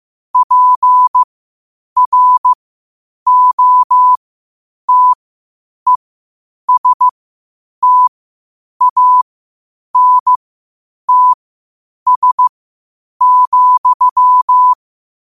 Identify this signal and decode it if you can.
Morse code, heard as a keyed tone, about 15 wpm, spaced as for 8 wpm, 1000 Hz. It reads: PROTESTANTS,
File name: qotdfiles / 8wpm